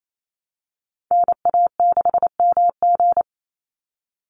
Morse code, heard as a keyed tone, about 28 wpm, 700 Hz. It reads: NA6MG
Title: NA SOTA Activator Callsigns 28wpm